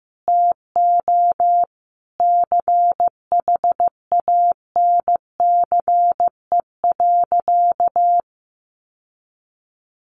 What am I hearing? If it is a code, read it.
Morse code, heard as a keyed tone, about 14 wpm, 700 Hz. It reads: TO CHANCE.
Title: Sentences from Top 1100 Words 15wpm